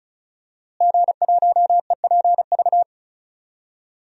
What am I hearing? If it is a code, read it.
Morse code, heard as a keyed tone, about 34 wpm, 700 Hz. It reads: G1EPV
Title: DXCC-100 Call Signs - 5 Characters - Repeated 2x - 35wpm